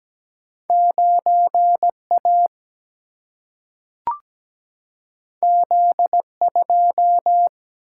Morse code, heard as a keyed tone, about 17 wpm, 700 Hz. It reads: 9A Z2
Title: Single-Letter Single-Number 17wpm